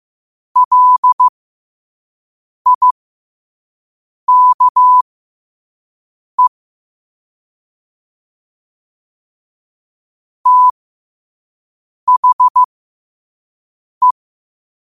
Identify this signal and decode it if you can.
Morse code, heard as a keyed tone, about 15 wpm, spaced as for 5 wpm, 1000 Hz. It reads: LIKE THE
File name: qotdfiles / 5wpm